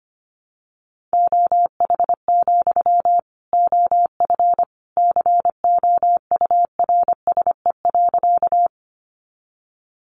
Morse code, heard as a keyed tone, about 25 wpm, 700 Hz. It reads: OH, OF COURSE.
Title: Sentences from Top 500 Words 25wpm